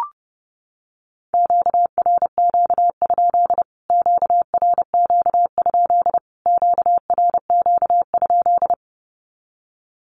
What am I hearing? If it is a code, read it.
Morse code, heard as a keyed tone, about 28 wpm, 700 Hz. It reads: QRQ? QRQ? QRQ?
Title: Q-Codes Repeated 3x 30wpm